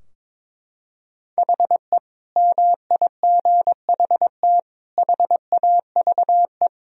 Morse code, heard as a keyed tone, about 21 wpm, 700 Hz. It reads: HE MIGHT HAVE
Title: Sentences from Top 1000 Words - Review Part 3 - 22wpm